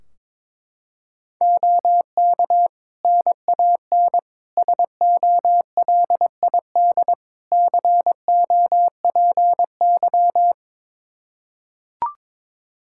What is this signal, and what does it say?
Morse code, heard as a keyed tone, about 22 wpm, 700 Hz. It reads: OK NAN SOLID COPY
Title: Practice QSO - Exchange 3 - More Info - 22wpm